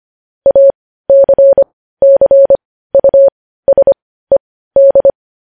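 Morse code, heard as a keyed tone, about 25 wpm, spaced as for 15 wpm, 550 Hz. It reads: ACCUSED
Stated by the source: Morse code news headlines sent at 15 WPM.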